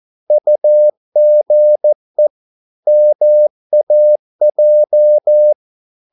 Morse code, heard as a keyed tone, about 14 wpm, 600 Hz. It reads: UGE MAJ
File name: Korte ord DK 14wpm